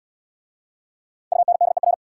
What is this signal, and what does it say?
Morse code, heard as a keyed tone, about 69 wpm, 690 Hz. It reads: UNDER